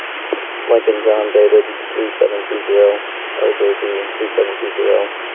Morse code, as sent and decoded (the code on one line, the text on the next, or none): none